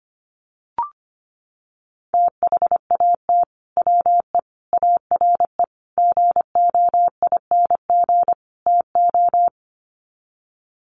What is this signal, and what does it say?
Morse code, heard as a keyed tone, about 25 wpm, 700 Hz. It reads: THAT WE ARE GOING TO
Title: Sets of 5 Words Encore 25wpm